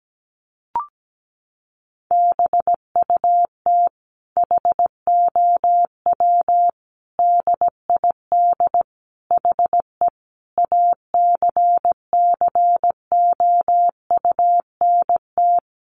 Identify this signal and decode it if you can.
Morse code, heard as a keyed tone, about 17 wpm, 700 Hz. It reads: BUT HOW DID HE ACCOUNT
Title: Sentences from Top 900 Words 17wpm